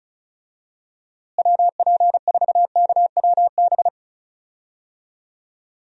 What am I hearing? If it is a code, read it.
Morse code, heard as a keyed tone, about 34 wpm, 700 Hz. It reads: WP4KWB
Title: USA Call Signs - 6 Characters - Repeated 3x - 35wpm